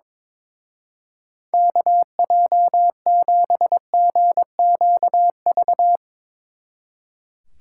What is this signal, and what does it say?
Morse code, heard as a keyed tone, about 24 wpm, 700 Hz. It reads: KJ7GQV
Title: Speed Racing - USA Call Signs 22wpm